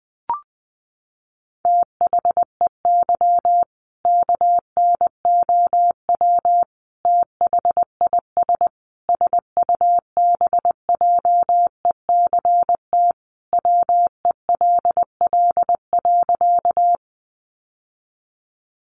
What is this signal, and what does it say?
Morse code, heard as a keyed tone, about 20 wpm, 700 Hz. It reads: THEY KNOW THIS SUBJECT WELL.
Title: Sentences from Top 800 Words 20wpm